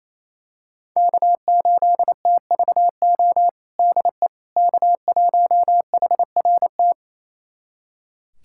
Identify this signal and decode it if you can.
Morse code, heard as a keyed tone, about 27 wpm, 700 Hz. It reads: K8TVO DE K1HRT
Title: USA Callsign DE Callsign - 5 Characters - 28wpm